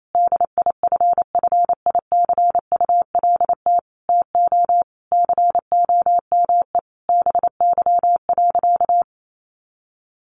Morse code, heard as a keyed tone, about 28 wpm, 700 Hz. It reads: DIFFICULT TO COME BY.
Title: Sentences from Top 1500 Words 28wpm